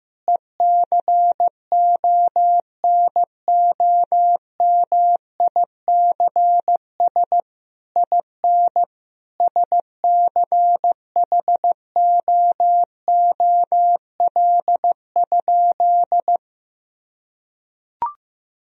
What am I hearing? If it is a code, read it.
Morse code, heard as a keyed tone, about 15 wpm, 700 Hz. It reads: ECONOMICS IN SCHOOL?
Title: Sentences from Top 1000 Words - Review Part 3 - 15wpm